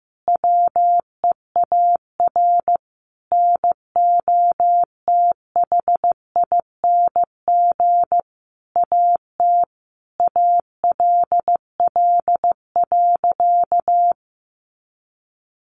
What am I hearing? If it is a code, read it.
Morse code, heard as a keyed tone, about 15 wpm, 700 Hz. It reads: WEAR NOTHING AT ALL.
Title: Sentences from Top 1500 Words 15wpm